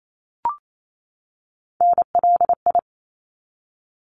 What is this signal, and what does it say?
Morse code, heard as a keyed tone, about 28 wpm, 700 Hz. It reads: NLI
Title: ARRL Sections 28wpm